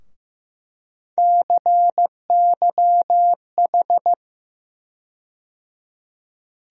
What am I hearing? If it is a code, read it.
Morse code, heard as a keyed tone, about 15 wpm, 700 Hz. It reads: CYH
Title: USA Call Sign Postfixes - 3 Letters 15wpm